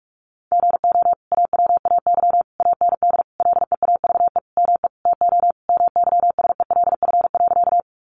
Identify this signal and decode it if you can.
Morse code, heard as a keyed tone, about 44 wpm, 700 Hz. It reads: GO AWAY AND LEAVE ME TO MYSELF.
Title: Sentences from Top 1300 Words 45wpm